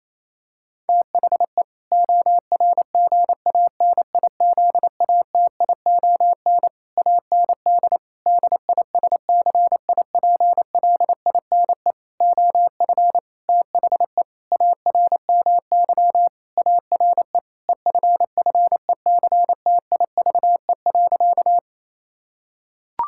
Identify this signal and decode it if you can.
Morse code, heard as a keyed tone, about 28 wpm, 700 Hz. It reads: THE ORGANIZATION AND DISCIPLINE OF THE ARMY ARE EFFECTIVE.
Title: Sentences from Top 1000 Words - Review Part 6 - 28wpm